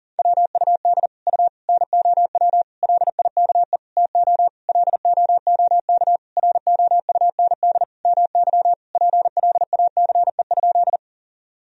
Morse code, heard as a keyed tone, about 37 wpm, 700 Hz. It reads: WUD U NOW LIKE TO LOOK ROUND MY PLACE?
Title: Sentences from Top 200 Words and Abbreviations 40wpm